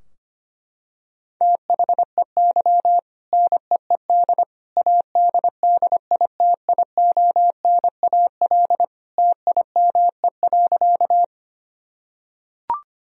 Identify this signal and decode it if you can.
Morse code, heard as a keyed tone, about 25 wpm, 700 Hz. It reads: THEY NEED ADDITIONAL TIME.
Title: Sentences from Top 1000 Words - Review Part 4 - 25wpm